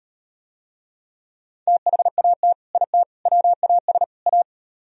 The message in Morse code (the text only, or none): THAT IT WAS A